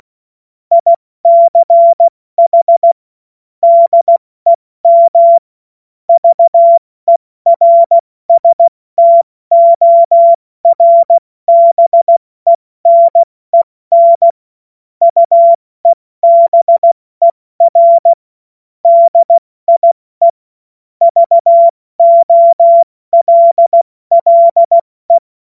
Morse code, heard as a keyed tone, about 16 wpm, 680 Hz. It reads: ICH DEM VERSTORBENEN UEBER DIE VOLLE
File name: Morsekurs / Witze / Witz 06. September 2022 70 F80